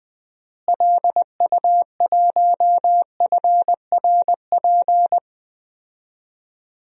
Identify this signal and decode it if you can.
Morse code, heard as a keyed tone, about 20 wpm, 700 Hz. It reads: LU1FRP
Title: DXCC-100 Call Signs - 6 Characters - Repeated 2x - 20wpm